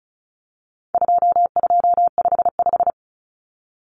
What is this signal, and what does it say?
Morse code, heard as a keyed tone, about 34 wpm, 700 Hz. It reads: 2255
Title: Numbers - 4-Digit 35wpm